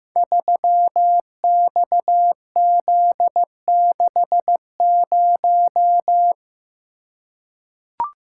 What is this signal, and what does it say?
Morse code, heard as a keyed tone, about 15 wpm, 700 Hz. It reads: 3XZ60
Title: Code Groups - 5 Letter:Numbers - 15wpm